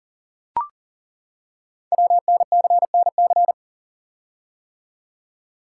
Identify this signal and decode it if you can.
Morse code, heard as a keyed tone, about 38 wpm, 700 Hz. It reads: WNCNC
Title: Code Groups - 5 Letters - 40wpm